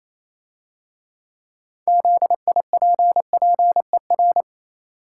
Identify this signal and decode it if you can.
Morse code, heard as a keyed tone, about 27 wpm, 700 Hz. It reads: ZIPPER
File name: Easy Words - Speed-Racing with Hints - 22wpm